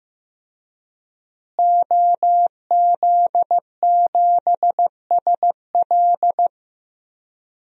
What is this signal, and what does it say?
Morse code, heard as a keyed tone, about 15 wpm, 700 Hz. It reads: OZ7SL
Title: DXCC-100 Call Signs - 5 Characters - Repeated 2x - 15wpm